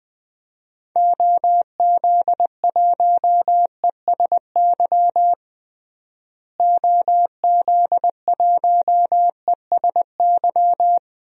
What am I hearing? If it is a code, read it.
Morse code, heard as a keyed tone, about 20 wpm, 700 Hz. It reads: OZ1ESY OZ1ESY
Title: DXCC-100 Call Signs - 6 Characters - Repeated 2x - 20wpm